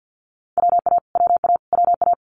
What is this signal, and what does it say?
Morse code, heard as a keyed tone, about 50 wpm, 690 Hz. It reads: WA WA WA
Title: Multi-letter Phonemes Repeated 3x 50wpm